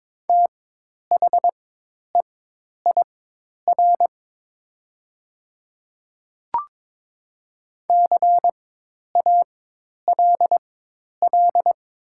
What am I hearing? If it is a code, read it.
Morse code, heard as a keyed tone, about 23 wpm, spaced as for 10 wpm, 690 Hz. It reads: THEIR CALL
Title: 4x Character Spacing - Top 300 Words - 22wpm